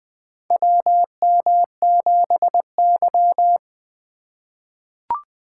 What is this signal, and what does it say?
Morse code, heard as a keyed tone, about 20 wpm, 700 Hz. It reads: WM7Y
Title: USA Call Signs - 4 Characters - 3x Character Spacing - 20wpm